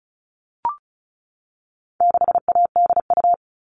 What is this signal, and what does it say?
Morse code, heard as a keyed tone, about 34 wpm, 700 Hz. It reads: 6ADU
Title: All Cut Numbers - 4-Digit 35wpm